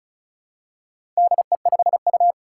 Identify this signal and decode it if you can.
Morse code, heard as a keyed tone, about 36 wpm, 700 Hz. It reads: DE5U
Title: Four Challenging Letter-Number Combinations 35wpm